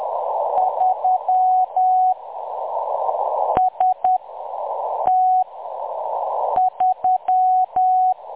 Morse code, heard as a keyed tone, about 10 wpm, 740 Hz. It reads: 3 S T 3
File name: M23-ST3- -0858utc-10184khz